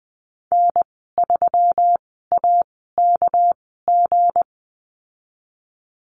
Morse code, heard as a keyed tone, about 20 wpm, spaced as for 15 wpm, 700 Hz. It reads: N3AKG
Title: 238 Lesson 38g - Letter q - Call Signs 20wpm